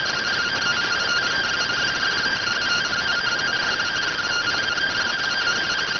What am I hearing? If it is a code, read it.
Two-tone teleprinter signal, about 170 Hz shift.